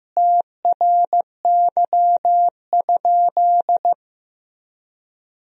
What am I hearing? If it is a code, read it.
Morse code, heard as a keyed tone, about 16 wpm, 700 Hz. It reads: TRY?
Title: Sentences from Top 200 Words and Abbreviations 15wpm